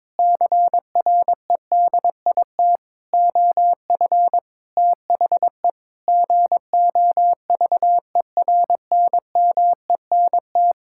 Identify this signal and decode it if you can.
Morse code, heard as a keyed tone, about 22 wpm, 700 Hz. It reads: CREDIT OF THE GOVERNMENT
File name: Sentences with 7-Word Phrases - 22wpm